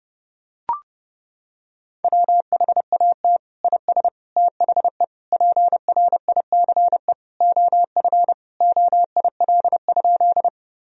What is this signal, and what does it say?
Morse code, heard as a keyed tone, about 29 wpm, 700 Hz. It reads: WHAT IS THE PRICE OF OIL?
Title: Sentences from Top 1000 Words - Review Part 4 - 30wpm